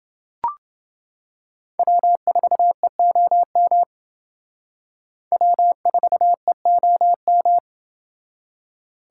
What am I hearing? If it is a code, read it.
Morse code, heard as a keyed tone, about 28 wpm, 700 Hz. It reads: W4EOM W4EOM